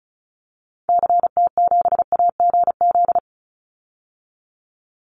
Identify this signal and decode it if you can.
Morse code, heard as a keyed tone, about 34 wpm, 700 Hz. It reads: CT7AGZ
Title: Call Signs - 35wpm